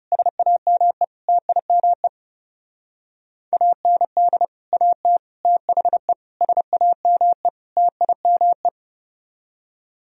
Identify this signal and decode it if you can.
Morse code, heard as a keyed tone, about 38 wpm, 700 Hz. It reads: SAME TIME AND AT THE SAME TIME